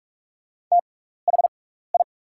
Morse code, heard as a keyed tone, about 44 wpm, spaced as for 15 wpm, 690 Hz. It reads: THI
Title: 6x Character Spacing - Top 100 Words - 45wpm